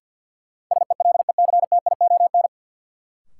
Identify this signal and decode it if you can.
Morse code, heard as a keyed tone, about 53 wpm, 700 Hz. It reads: SELECTION
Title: Sentences from Top 1400 Words - New Word Drill - 50wpm